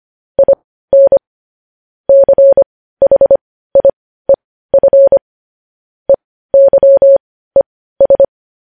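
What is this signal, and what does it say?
Morse code, heard as a keyed tone, about 25 wpm, spaced as for 15 wpm, 550 Hz. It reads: IN CHIEF EYES
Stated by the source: News headlines sent in Morse code at 15 words per minute, preceded by vvv.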